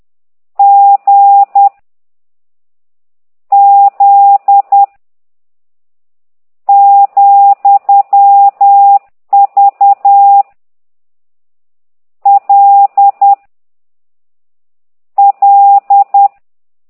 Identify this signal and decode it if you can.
Morse code, heard as a keyed tone, about 10 wpm, 800 Hz. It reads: G Z , V L L
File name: ct2idw / exerc15